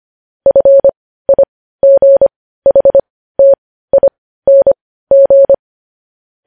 Morse code, heard as a keyed tone, about 25 wpm, spaced as for 15 wpm, 550 Hz. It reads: FIGHTING
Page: news headlines in morse code at 15 wpm